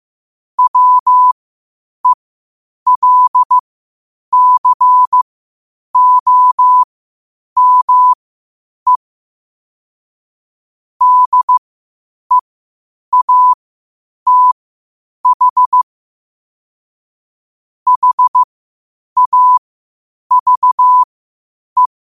Morse code, heard as a keyed tone, about 15 wpm, spaced as for 8 wpm, 1000 Hz. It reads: WELCOME DEATH HAVE
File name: qotdfiles / 8wpm